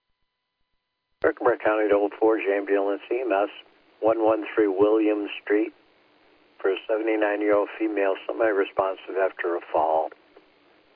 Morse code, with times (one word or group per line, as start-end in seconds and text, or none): none